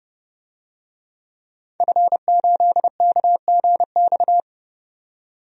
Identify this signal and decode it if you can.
Morse code, heard as a keyed tone, about 30 wpm, 700 Hz. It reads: F8KGX
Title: DXCC-100 Call Signs - 5 Characters - 30wpm